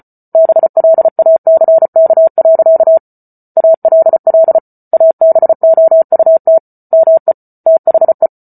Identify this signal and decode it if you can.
Morse code, heard as a keyed tone, about 34 wpm, 650 Hz. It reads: BLACK. ALL ABOUT ME THE